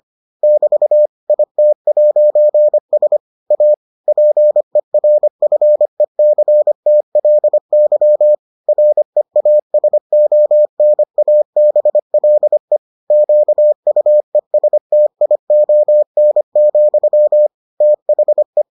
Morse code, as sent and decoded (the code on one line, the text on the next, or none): -...- / .. - .----. ... / .- / .--. .-. ..-. -.-. - .-.. -.-- / .-. .- ... --- -. .- -... .-.. / --.- ..- . ... - .. --- -. --..-- / - .... .
<BT> IT'S A PERFECTLY REASONABLE QUESTION, THE